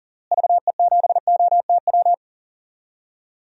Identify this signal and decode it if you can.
Morse code, heard as a keyed tone, about 40 wpm, 700 Hz. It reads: VE7OTW
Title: DXCC-100 Call Signs - 6 Characters - Repeated 3x - 40wpm